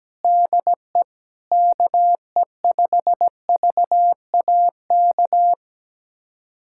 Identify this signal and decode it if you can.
Morse code, heard as a keyed tone, about 17 wpm, 700 Hz. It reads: DE KE5VAK
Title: USA Callsign DE Callsign 17wpm